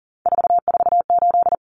Morse code, heard as a keyed tone, about 40 wpm, 700 Hz. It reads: 448
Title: Numbers - 3-Digit 40wpm